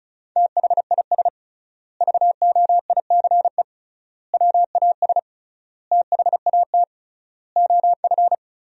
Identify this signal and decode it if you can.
Morse code, heard as a keyed tone, about 36 wpm, 700 Hz. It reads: THIS VOICE WAS THAT OF
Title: 3x Word-Spacing - Sentences from Top 500 Words - 35wpm